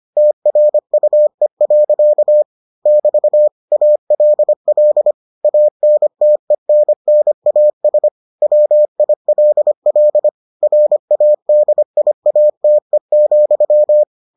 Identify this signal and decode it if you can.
Morse code, heard as a keyed tone, about 25 wpm, 600 Hz. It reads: TRUE. <BT> ALL ANTENNAS WILL RADIATE,